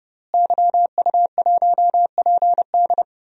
Morse code, heard as a keyed tone, about 30 wpm, 700 Hz. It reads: YU1PD
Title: DXCC-100 Call Signs - 5 Characters - Repeated 2x - 30wpm